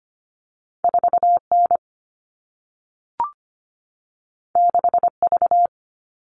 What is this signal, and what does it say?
Morse code, heard as a keyed tone, about 25 wpm, 700 Hz. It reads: 4 N 6 V
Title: All Cut Numbers - 2-Digit 25wpm